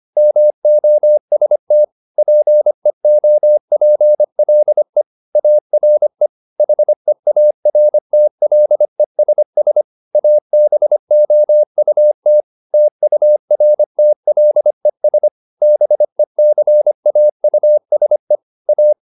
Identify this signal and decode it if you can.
Morse code, heard as a keyed tone, about 25 wpm, 600 Hz. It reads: MOST PEOPLE ARE HEARTLESS ABOUT TURTLES BECAUSE A
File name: the old man and the sea part 1 25wpm